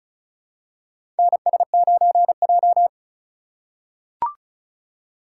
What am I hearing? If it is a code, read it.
Morse code, heard as a keyed tone, about 36 wpm, 700 Hz. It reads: NS9J
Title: USA Call Signs - 4 Characters - 4x Character Spacing - 35wpm